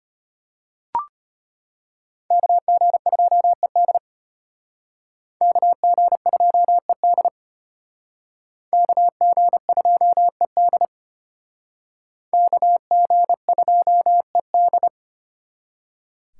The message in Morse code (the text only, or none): KG2ED KG2ED KG2ED KG2ED